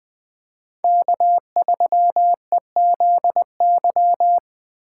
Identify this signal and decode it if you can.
Morse code, heard as a keyed tone, about 20 wpm, 700 Hz. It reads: K3EZY